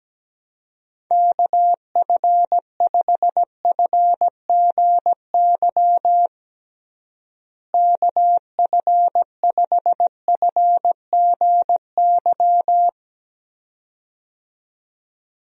Morse code, heard as a keyed tone, about 17 wpm, 700 Hz. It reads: KF5FGY KF5FGY
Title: USA Call Signs - 6 Characters - Repeated 2x - 17wpm